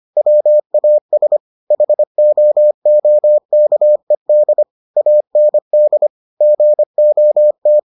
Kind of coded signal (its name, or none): Morse code